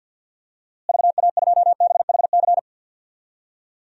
Morse code, heard as a keyed tone, about 49 wpm, 700 Hz. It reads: VA2BHC
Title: DXCC-100 Call Signs - 6 Characters - Repeated 2x - 50wpm